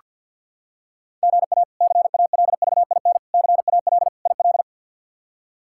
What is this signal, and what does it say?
Morse code, heard as a keyed tone, about 49 wpm, 700 Hz. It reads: GA CALVIN CAL IL
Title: K1USN SST QSO Practice - 50wpm